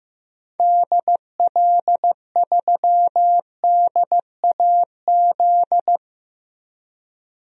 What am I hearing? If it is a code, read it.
Morse code, heard as a keyed tone, about 15 wpm, 700 Hz. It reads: DL3DAZ
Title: DXCC-100 Call Signs - 6 Characters - 15wpm